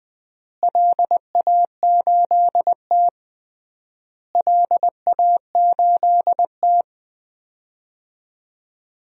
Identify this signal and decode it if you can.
Morse code, heard as a keyed tone, about 20 wpm, 700 Hz. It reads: LA8T LA8T
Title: DXCC-100 Call Signs - 4 Characters - Repeated 3x - 20wpm